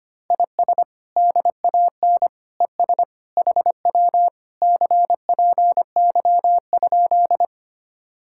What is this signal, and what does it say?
Morse code, heard as a keyed tone, about 24 wpm, 700 Hz. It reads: IS DAN ES HW CPY?
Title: Speed-Racing Common QSO Phrases - 25wpm